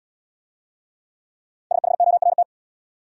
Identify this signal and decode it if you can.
Morse code, heard as a keyed tone, about 68 wpm, 700 Hz. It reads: SHORT